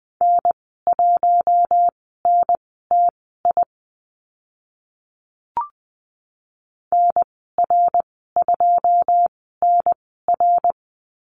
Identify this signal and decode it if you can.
Morse code, heard as a keyed tone, about 20 wpm, spaced as for 14 wpm, 700 Hz. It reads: N1NTI NR2NR
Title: 082 Lesson 14g - Number 2 - Call Signs 20wpm